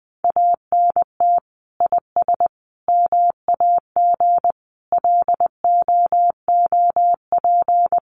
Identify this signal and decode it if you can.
Morse code, heard as a keyed tone, about 19 wpm, 700 Hz. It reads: ANT IS MAG LOOP
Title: Common QSO Phrases 20wpm